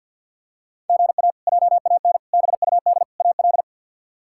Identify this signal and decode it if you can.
Morse code, heard as a keyed tone, about 46 wpm, 700 Hz. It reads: GA JAN BUD AL
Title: K1USN SST QSO Practice - 50wpm